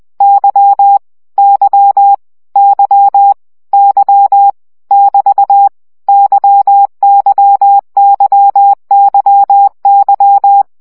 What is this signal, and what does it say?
Morse code, heard as a keyed tone, about 20 wpm, 800 Hz. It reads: Y Y Y Y <BT> YYYYY
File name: CW gyakorlas y 17 baud